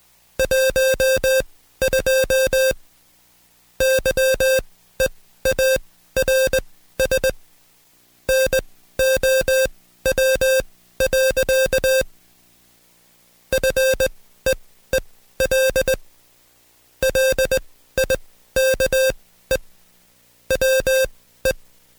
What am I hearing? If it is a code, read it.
Morse code, heard as a keyed tone, about 19 wpm, spaced as for 13 wpm, 530 Hz. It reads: 12 YEARS NOW. FEEL LIKE WE
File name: Code Quick / AudioDownld2 / Seg133-Stepper6 13WPM